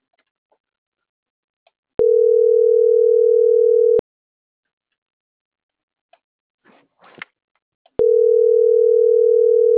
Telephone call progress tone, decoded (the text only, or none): ringback tone